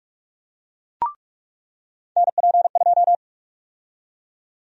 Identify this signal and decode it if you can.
Morse code, heard as a keyed tone, about 47 wpm, 700 Hz. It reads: NP2
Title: USA Call Sign Prefixes - 45wpm